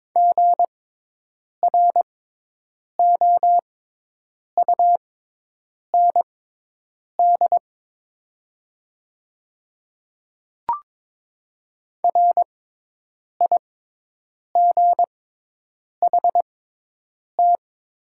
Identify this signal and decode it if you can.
Morse code, heard as a keyed tone, about 22 wpm, spaced as for 8 wpm, 700 Hz. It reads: GROUND RIGHT
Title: 6x Character Spacing - Top 500 Words - 22wpm